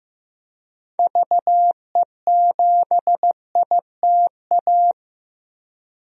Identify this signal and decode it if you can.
Morse code, heard as a keyed tone, about 15 wpm, 700 Hz. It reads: VE7ITA
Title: DXCC-100 Call Signs - 6 Characters - Repeated 2x - 15wpm